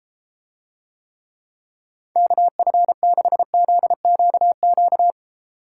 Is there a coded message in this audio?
Yes: Morse code from 2.2 to 5.1 s, heard as a keyed tone, about 33 wpm, 700 Hz. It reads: KF6ZQQ